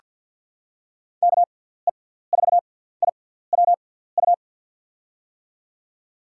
Morse code, heard as a keyed tone, about 47 wpm, spaced as for 17 wpm, 700 Hz. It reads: KE4IWU
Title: USA Call Signs - 6x Character Spacing - 50wpm